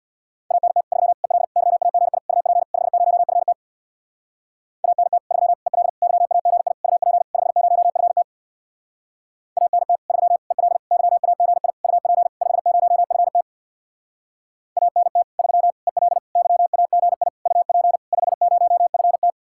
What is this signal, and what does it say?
Morse code, heard as a keyed tone, about 58 wpm, 700 Hz. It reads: ANT 3 EL YAGI UP 50FT ANT 3 EL YAGI UP 50FT ANT 3 EL YAGI UP 50FT ANT 3 EL YAGI UP 50FT
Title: Speed-Racing Common QSO Phrases - 50wpm